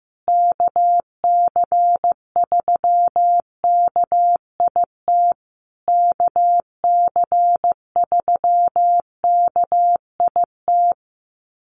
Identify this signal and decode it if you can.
Morse code, heard as a keyed tone, about 15 wpm, 700 Hz. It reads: KC3KIT KC3KIT
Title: USA Call Signs Repeated 3x 15wpm